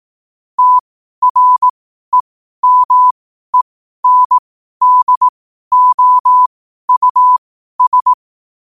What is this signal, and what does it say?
Morse code, heard as a keyed tone, about 18 wpm, spaced as for 12 wpm, 1000 Hz. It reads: TREMENDOUS
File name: qotdfiles / 12wpm